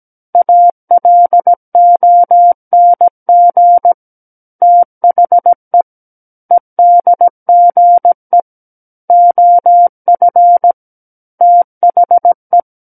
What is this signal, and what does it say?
Morse code, heard as a keyed tone, about 17 wpm, 700 Hz. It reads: ALONG THE EDGE OF THE
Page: SKCC Learning Center - Morse Code Practice Files